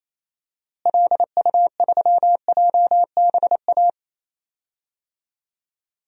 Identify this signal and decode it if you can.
Morse code, heard as a keyed tone, about 27 wpm, 700 Hz. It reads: LU3JBA
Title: DXCC-100 Call Signs - 6 Characters - Repeated 2x - 28wpm